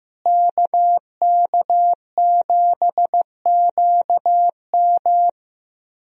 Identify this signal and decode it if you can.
Morse code, heard as a keyed tone, about 15 wpm, 700 Hz. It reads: KK7QM
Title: USA Call Signs - 5 Characters - Repeated 3x - 15wpm